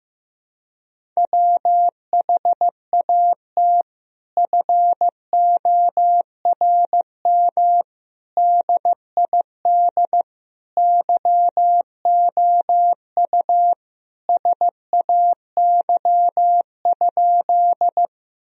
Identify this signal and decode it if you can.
Morse code, heard as a keyed tone, about 15 wpm, 700 Hz. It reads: WHAT FORM DID YOU SAY?
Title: Sentences from Top 1000 Words - Review Part 1 - 15wpm